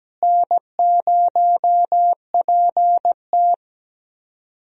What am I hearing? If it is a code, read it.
Morse code, heard as a keyed tone, about 17 wpm, 700 Hz. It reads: N0PT
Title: USA Call Signs - 4 Characters - Repeated 3x - 17wpm